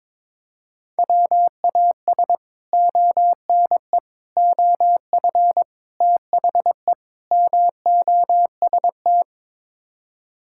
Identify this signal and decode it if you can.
Morse code, heard as a keyed tone, about 22 wpm, 700 Hz. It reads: WAS ONE OF THE MOST
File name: practice / 22wpm / Build A Phrase - 5-Word Phrases - 22wpm